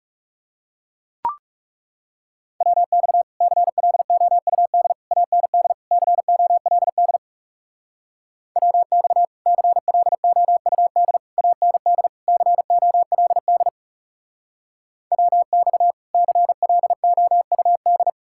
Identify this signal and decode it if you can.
Morse code, heard as a keyed tone, about 39 wpm, 700 Hz. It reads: WX CLOUD AND COLD WX CLOUD AND COLD WX CLOUD